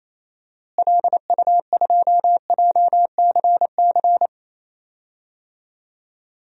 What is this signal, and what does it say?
Morse code, heard as a keyed tone, about 28 wpm, 700 Hz. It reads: LU2JCC
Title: DXCC-100 Call Signs - 6 Characters - Repeated 2x - 28wpm